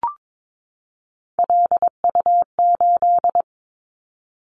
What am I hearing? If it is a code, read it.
Morse code, heard as a keyed tone, about 22 wpm, 700 Hz. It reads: LU8
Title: Callsign Prefixes for Top 50 DXCC - 22wpm